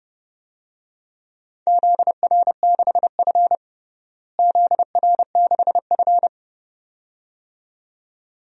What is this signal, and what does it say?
Morse code, heard as a keyed tone, about 30 wpm, 700 Hz. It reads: ZR6F ZR6F
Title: DXCC-100 Call Signs - 4 Characters - Repeated 2x - 30wpm